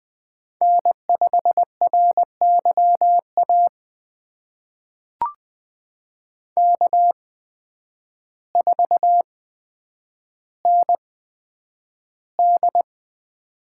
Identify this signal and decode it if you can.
Morse code, heard as a keyed tone, about 21 wpm, spaced as for 9 wpm, 700 Hz. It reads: N5RYA K4ND
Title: USA Call Signs - 8x Character Spacing - 20wpm